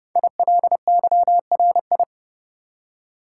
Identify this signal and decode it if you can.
Morse code, heard as a keyed tone, about 30 wpm, 700 Hz. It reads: ILYRI
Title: Code Groups - 5 Letters - 30wpm